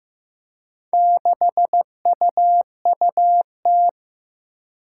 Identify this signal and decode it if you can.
Morse code, heard as a keyed tone, about 15 wpm, 700 Hz. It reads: 6UUT
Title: Four Challenging Letter-Number Combinations 15wpm